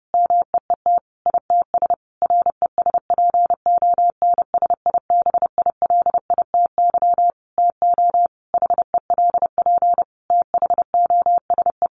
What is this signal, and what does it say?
Morse code, heard as a keyed tone, about 30 wpm, 700 Hz. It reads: MEET ITS RESPONSIBILITY TO HELP THOSE
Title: Sentences from Top 1200 Words 30wpm